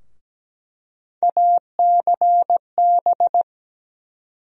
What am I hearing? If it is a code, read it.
Morse code, heard as a keyed tone, about 17 wpm, 700 Hz. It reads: ACB
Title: USA Call Sign Postfixes - 3 Letters 17wpm